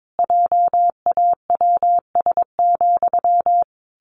Morse code, heard as a keyed tone, about 22 wpm, 700 Hz. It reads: JAWS,
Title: A Sound Of Thunder 22wpm